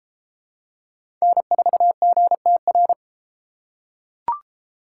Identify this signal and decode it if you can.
Morse code, heard as a keyed tone, about 28 wpm, 700 Hz. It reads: N4GTR